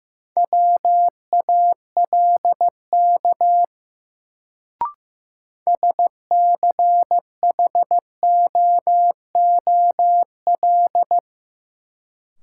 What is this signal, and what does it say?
Morse code, heard as a keyed tone, about 15 wpm, 700 Hz. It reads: WALK SCHOOL
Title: Sentences from Top 300 Words - New Word Drill - 15wpm